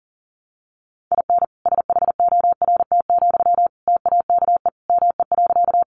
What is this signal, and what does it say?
Morse code, heard as a keyed tone, about 40 wpm, 700 Hz. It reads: IN SHORT, TAKE ME.
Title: Sentences from Top 400 Words 40wpm